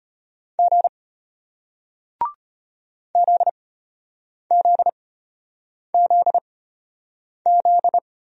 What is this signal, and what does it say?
Morse code, heard as a keyed tone, about 33 wpm, spaced as for 8 wpm, 700 Hz. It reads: G ZZZZ